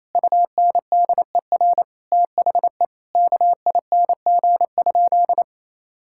Morse code, heard as a keyed tone, about 28 wpm, 700 Hz. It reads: UNDER THE KING?
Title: Sentences from Top 1000 Words - Review Part 6 - 28wpm